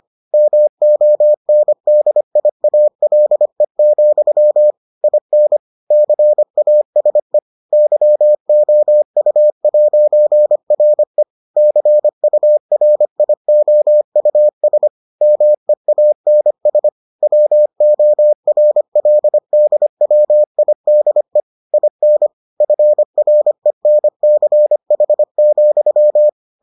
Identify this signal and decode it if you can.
Morse code, heard as a keyed tone, about 25 wpm, 600 Hz. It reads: MONDIALE, IN CASE YOU'RE CURIOUS MEANS WORLDWIDE IN FRENCH,